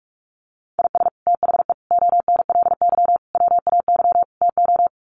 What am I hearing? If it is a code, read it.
Morse code, heard as a keyed tone, about 44 wpm, 700 Hz. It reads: IS THE ONLY WAY TO
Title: Sets of 5 Words Encore 45wpm